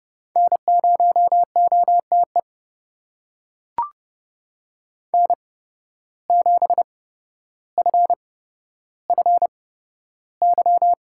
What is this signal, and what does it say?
Morse code, heard as a keyed tone, about 30 wpm, spaced as for 13 wpm, 700 Hz. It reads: N0OTE N7FFY
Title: USA Call Signs - 8x Character Spacing - 30wpm